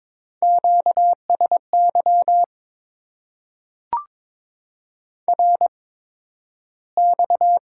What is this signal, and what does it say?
Morse code, heard as a keyed tone, about 22 wpm, spaced as for 12 wpm, 700 Hz. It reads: QSY RX
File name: Top 90 QSO Elements - 8x Character Spacing - 22wpm